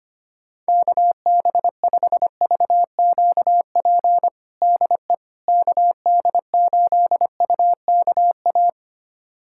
Morse code, heard as a keyed tone, about 25 wpm, 700 Hz. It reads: KB5VQP DE KD8UKA
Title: USA Callsign DE Callsign - 6 Characters - 25wpm